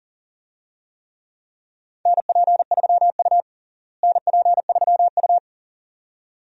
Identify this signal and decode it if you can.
Morse code, heard as a keyed tone, about 40 wpm, 700 Hz. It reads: NP3U NP3U
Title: USA Call Signs - 4 Characters - Repeated 2x - 40wpm